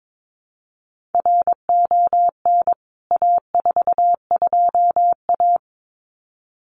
Morse code, heard as a keyed tone, about 22 wpm, 700 Hz. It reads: RON A42A
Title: CWT Exchange 22wpm